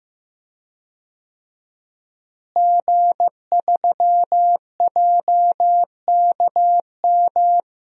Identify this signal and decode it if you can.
Morse code, heard as a keyed tone, about 15 wpm, 700 Hz. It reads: G3JKM
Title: DXCC-100 Call Signs - 5 Characters - Repeated 2x - 15wpm